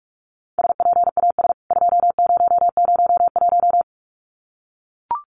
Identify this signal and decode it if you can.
Morse code, heard as a keyed tone, about 44 wpm, 700 Hz. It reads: SPAS 2001
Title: CWT Exchange 45wpm